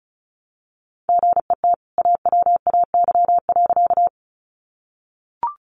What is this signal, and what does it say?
Morse code, heard as a keyed tone, about 34 wpm, 700 Hz. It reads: GET AWAY.
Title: Sentences from Top 600 Words 35wpm